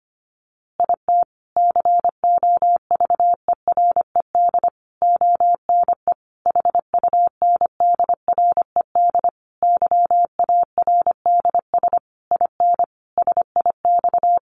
Morse code, heard as a keyed tone, about 25 wpm, 700 Hz. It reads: IT COVERED ONE HUNDRED YARDS IN SIX
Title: A Sound Of Thunder 25wpm